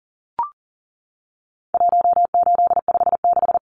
Morse code, heard as a keyed tone, about 40 wpm, 700 Hz. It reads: 1856
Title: Numbers - 4-Digit 40wpm